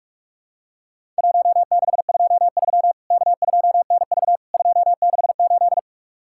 Morse code, heard as a keyed tone, about 44 wpm, 700 Hz. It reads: 1623 K2NV 268
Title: Worked All Europe DX Contest - QTC Exchange - 45wpm